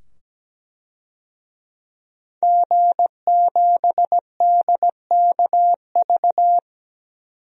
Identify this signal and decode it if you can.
Morse code, heard as a keyed tone, about 17 wpm, 700 Hz. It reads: G7DKV
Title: DXCC-100 Call Signs - 5 Characters - 17wpm